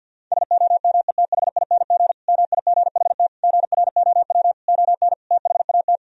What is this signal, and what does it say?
Morse code, heard as a keyed tone, about 51 wpm, 700 Hz. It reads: SOMETHING MIGHT GROW ON THAT
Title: Sentences from Top 1000 Words - Review Part 1 - 50wpm